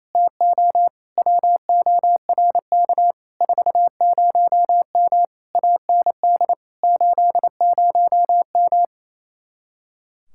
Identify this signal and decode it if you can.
Morse code, heard as a keyed tone, about 26 wpm, 700 Hz. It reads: TO WORK 40M AND 80M
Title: Speed-Racing Common QSO Phrases - 28wpm